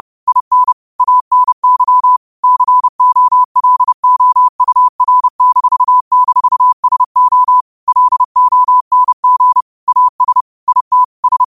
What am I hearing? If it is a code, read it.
Morse code, heard as a keyed tone, about 30 wpm, 1000 Hz. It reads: IN ANY COLOUR--SO LONG AS IT S
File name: qotdfiles / 30wpm